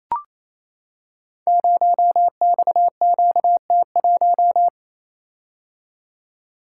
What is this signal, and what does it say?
Morse code, heard as a keyed tone, about 28 wpm, 700 Hz. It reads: 0XQT1
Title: Code Groups - 5 Letter:Numbers - 28wpm